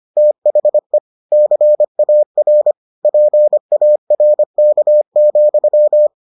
Morse code, heard as a keyed tone, about 25 wpm, 600 Hz. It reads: THE CAR PARK,